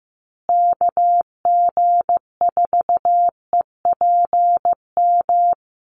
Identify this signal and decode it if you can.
Morse code, heard as a keyed tone, about 15 wpm, 700 Hz. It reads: KG4EPM
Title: USA Call Signs Repeated 2x 15wpm